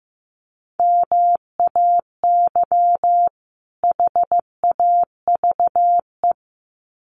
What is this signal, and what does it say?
Morse code, heard as a keyed tone, about 15 wpm, 700 Hz. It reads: MAY HAVE
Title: Sets of 2 Words 15wpm